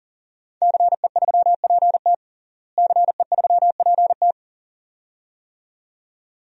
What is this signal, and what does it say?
Morse code, heard as a keyed tone, about 40 wpm, 700 Hz. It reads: CE3PT CE3PT
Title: DXCC-100 Call Signs - 5 Characters - Repeated 2x - 40wpm